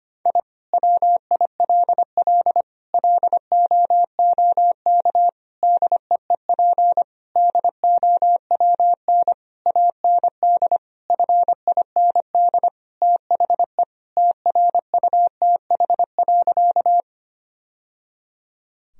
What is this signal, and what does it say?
Morse code, heard as a keyed tone, about 25 wpm, 700 Hz. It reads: I WILL LOOK DEEP DOWN AND FIND THE TRUTH.
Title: Sentences from Top 1000 Words - Review Part 3 - 25wpm